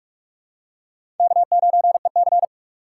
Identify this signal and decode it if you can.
Morse code, heard as a keyed tone, about 42 wpm, 700 Hz. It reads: K9EC
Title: USA Call Signs - 3x Character Spacing - 45wpm